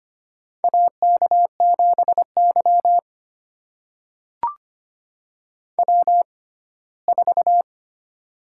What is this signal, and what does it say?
Morse code, heard as a keyed tone, about 25 wpm, spaced as for 17 wpm, 700 Hz. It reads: AK7Y W4
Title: USA Call Signs - 4 Characters - 6x Character Spacing - 25wpm